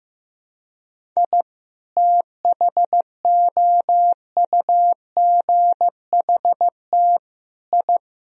Morse code, heard as a keyed tone, about 15 wpm, 700 Hz. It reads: I THOUGHT I
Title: Sentences from Top 1000 Words - Review Part 2 - 15wpm